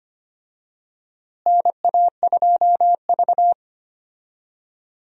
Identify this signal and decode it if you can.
Morse code, heard as a keyed tone, about 25 wpm, 700 Hz. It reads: NA2V
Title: USA Call Signs - 4 Characters - Repeated 3x - 25wpm